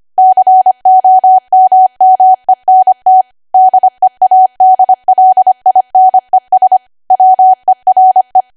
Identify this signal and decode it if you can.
Morse code, heard as a keyed tone, about 25 wpm, 750 Hz. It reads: COMMENT DEADLINES WERE
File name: cwa / W1AW-Level3 / 25WPM